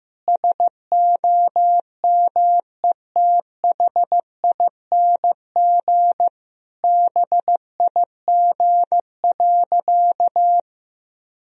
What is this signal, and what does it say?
Morse code, heard as a keyed tone, about 15 wpm, 700 Hz. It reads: SOMETHING BIG.
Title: Sentences from Top 1000 Words - Review Part 1 - 15wpm